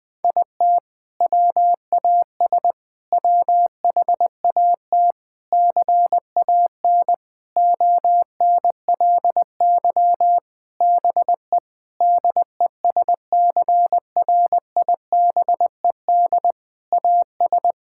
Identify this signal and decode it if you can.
Morse code, heard as a keyed tone, about 21 wpm, 700 Hz. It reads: IT WAS WHAT CAN ONLY BE DESCRIBED AS
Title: Sentences with 6-Word Phrases - 20wpm